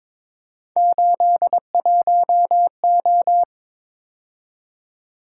Morse code, heard as a keyed tone, about 22 wpm, 700 Hz. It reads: 81O
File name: practice / 22wpm / Three Letter-Number Combinations - 22wpm